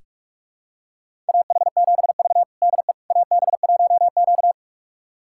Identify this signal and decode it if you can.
Morse code, heard as a keyed tone, about 41 wpm, 700 Hz. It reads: AH7V DE AB1Q
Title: USA Callsign DE Callsign - 4 Characters - 45wpm